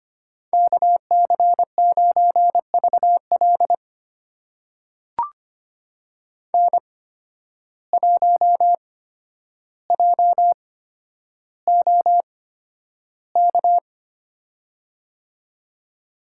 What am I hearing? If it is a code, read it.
Morse code, heard as a keyed tone, about 25 wpm, spaced as for 11 wpm, 700 Hz. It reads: KC9VL N1JOK
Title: USA Call Signs - 5 Characters - 8x Character Spacing - 25wpm